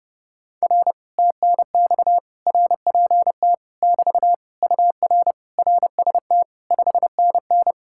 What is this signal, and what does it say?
Morse code, heard as a keyed tone, about 32 wpm, 700 Hz. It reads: R TNX RPT <BT> UR RST 5NN